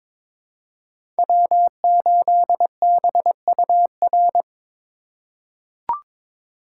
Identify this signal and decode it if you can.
Morse code, heard as a keyed tone, about 21 wpm, 700 Hz. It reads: W8BUR
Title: USA Call Signs - 4x Character Spacing - 22wpm